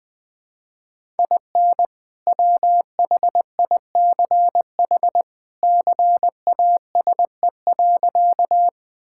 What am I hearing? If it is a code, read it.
Morse code, heard as a keyed tone, about 20 wpm, 700 Hz. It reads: IN WHICH CASE.
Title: Sentences with 2-Word Phrases - 20wpm